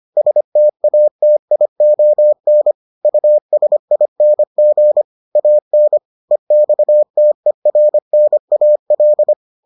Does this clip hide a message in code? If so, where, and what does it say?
Yes: Morse code from 0.2 to 9.3 s, heard as a keyed tone, about 25 wpm, 600 Hz. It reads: STATION USING AN EXTERNAL